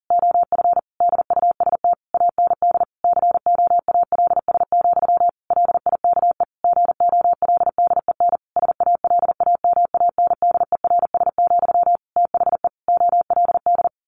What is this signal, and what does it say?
Morse code, heard as a keyed tone, about 40 wpm, 700 Hz. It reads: OF DUST AND COALS, LIKE GOLDEN SALAMANDERS, THE OLD
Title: A Sound Of Thunder 40wpm